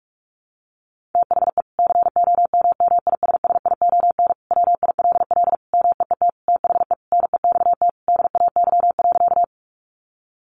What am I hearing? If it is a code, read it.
Morse code, heard as a keyed tone, about 44 wpm, 700 Hz. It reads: THE COMMISSION WILL MEET THE NEXT DAY.
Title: Sentences from Top 1400 Words 45wpm